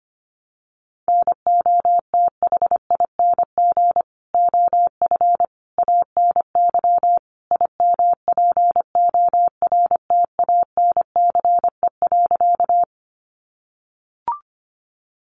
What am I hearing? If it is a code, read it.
Morse code, heard as a keyed tone, about 25 wpm, 700 Hz. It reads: NOTHING OF ANY IMPORTANCE.
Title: Sentences from Top 1200 Words 25wpm